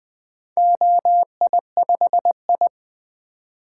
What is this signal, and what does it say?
Morse code, heard as a keyed tone, about 20 wpm, 700 Hz. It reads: OI5I
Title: Four Letter-Number Combinations - 20wpm